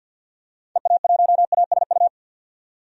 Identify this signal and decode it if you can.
Morse code, heard as a keyed tone, about 52 wpm, 700 Hz. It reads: EA1ASU
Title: DXCC-100 Call Signs - 6 Characters - 50wpm